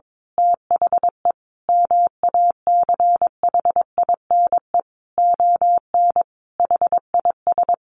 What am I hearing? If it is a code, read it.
Morse code, heard as a keyed tone, about 22 wpm, 700 Hz. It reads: THE MACHINE ON HIS
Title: A Sound Of Thunder 22wpm